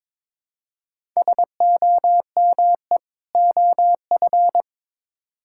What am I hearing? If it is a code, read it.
Morse code, heard as a keyed tone, about 21 wpm, 700 Hz. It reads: SOME OF
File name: Build A Phrase - 2-Word Phrases - 22wpm